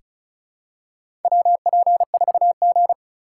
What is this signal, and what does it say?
Morse code, heard as a keyed tone, about 36 wpm, 690 Hz. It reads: WP4G
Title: USA Call Signs - 4x Character Spacing - 35wpm